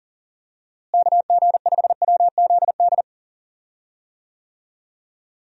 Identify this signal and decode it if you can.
Morse code, heard as a keyed tone, about 37 wpm, 700 Hz. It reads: KG5WZD
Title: USA Call Signs - 6 Characters - Repeated 2x - 40wpm